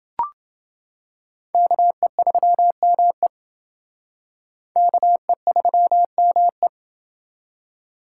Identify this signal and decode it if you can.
Morse code, heard as a keyed tone, about 28 wpm, 700 Hz. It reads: KE3ME KE3ME